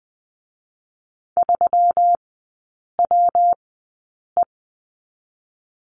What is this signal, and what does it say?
Morse code, heard as a keyed tone, about 20 wpm, spaced as for 8 wpm, 700 Hz. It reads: 3WE
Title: 121 Lesson 20e - Number 3 - Three Characters 20wpm